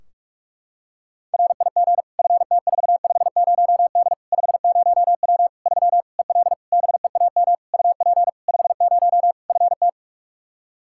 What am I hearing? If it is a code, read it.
Morse code, heard as a keyed tone, about 46 wpm, 700 Hz. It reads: RIG FT450D 50W 3 EL BEAM UP 50 FT
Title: K1USN SST QSO Practice 45wpm